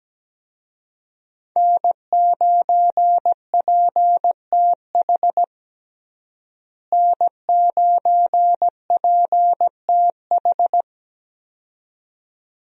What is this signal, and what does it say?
Morse code, heard as a keyed tone, about 17 wpm, 700 Hz. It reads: N9PTH N9PTH
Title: USA Call Signs - 5 Characters - Repeated 2x - 17wpm